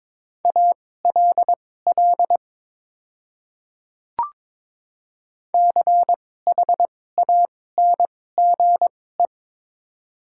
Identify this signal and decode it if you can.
Morse code, heard as a keyed tone, about 22 wpm, spaced as for 16 wpm, 700 Hz. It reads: ALL CHANGE
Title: 2x Character Spacing - Top 200 Words - 22wpm